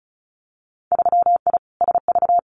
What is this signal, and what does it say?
Morse code, heard as a keyed tone, about 36 wpm, 700 Hz. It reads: 3I SV
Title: Winter Field Day Exchanges Encore 35wpm